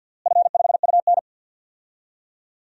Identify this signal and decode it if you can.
Morse code, heard as a keyed tone, about 47 wpm, 700 Hz. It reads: F5UN
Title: DXCC-100 Call Signs - 4 Characters - Repeated 2x - 50wpm